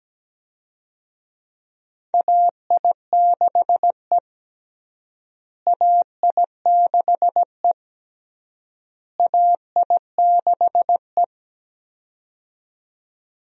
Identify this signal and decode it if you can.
Morse code, heard as a keyed tone, about 17 wpm, 700 Hz. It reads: AI6E AI6E AI6E
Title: USA Call Signs - 4 Characters - Repeated 3x - 17wpm